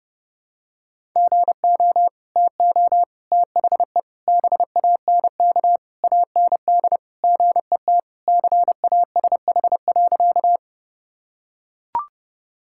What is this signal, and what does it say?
Morse code, heard as a keyed tone, about 32 wpm, 700 Hz. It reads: GO TO THE BANK AND GET CASH.
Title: Sentences from Top 1000 Words - Review Part 5 - 30wpm